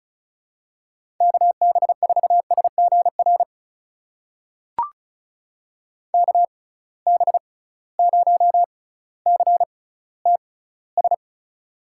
Morse code, heard as a keyed tone, about 36 wpm, spaced as for 18 wpm, 700 Hz. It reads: KB4SGR KB0CTS
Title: USA Call Signs - 6 Characters - 6x Character Spacing - 35wpm